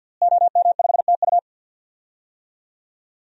Morse code, heard as a keyed tone, about 48 wpm, 700 Hz. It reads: OM5TU
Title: DXCC-100 Call Signs - 5 Characters - Repeated 3x - 50wpm